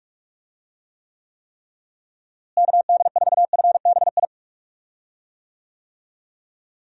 Morse code, heard as a keyed tone, about 46 wpm, 700 Hz. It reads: KD4FBI
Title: USA Call Signs - 6 Characters - Repeated 3x - 45wpm